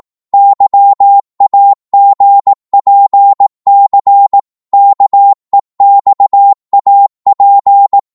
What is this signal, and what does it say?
Morse code, heard as a keyed tone, about 18 wpm, 800 Hz. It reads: YAGPC KEXAP
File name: Morse / ltr-18wpm